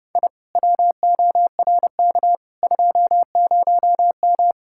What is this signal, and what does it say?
Morse code, heard as a keyed tone, about 30 wpm, 700 Hz. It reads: I WORK 20M